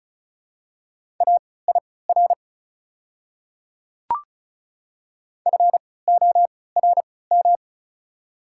Morse code, heard as a keyed tone, about 36 wpm, spaced as for 20 wpm, 700 Hz. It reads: AIR FORM
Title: 3x Character Spacing - Top 200 Words - 35wpm